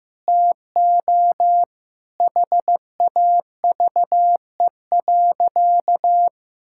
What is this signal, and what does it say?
Morse code, heard as a keyed tone, about 16 wpm, 700 Hz. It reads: TO HAVE.
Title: Sentences from Top 1000 Words - Review Part 5 - 15wpm